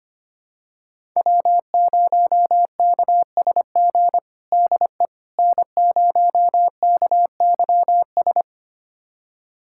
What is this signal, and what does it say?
Morse code, heard as a keyed tone, about 25 wpm, 700 Hz. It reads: W0KSG DE N0KYS
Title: USA Callsign DE Callsign - 5 Characters - 25wpm